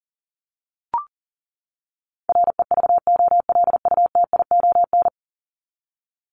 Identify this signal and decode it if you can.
Morse code, heard as a keyed tone, about 40 wpm, 700 Hz. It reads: REVOLUTION
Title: Top 1751-2000 Words 40wpm